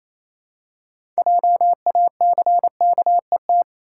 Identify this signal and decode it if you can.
Morse code, heard as a keyed tone, about 28 wpm, 700 Hz. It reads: JACKET
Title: Six Letter Words - 28wpm